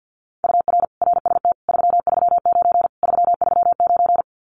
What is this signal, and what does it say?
Morse code, heard as a keyed tone, about 50 wpm, 700 Hz. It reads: UR RST 339 339
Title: Common QSO Phrases Encore 50wpm